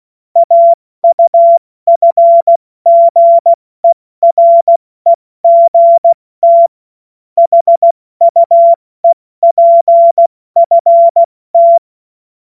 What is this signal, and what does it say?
Morse code, heard as a keyed tone, about 16 wpm, 680 Hz. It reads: AUFGEREGT HUEPFT
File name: Morsekurs / Witze / Witz 17. Mai 2022 70 F80